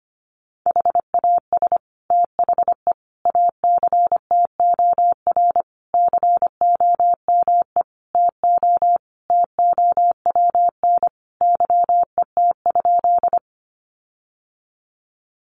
Morse code, heard as a keyed tone, about 25 wpm, 700 Hz. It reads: HAS THE ACTOR COME TO TOWN YET?
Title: Sentences from Top 1500 Words 25wpm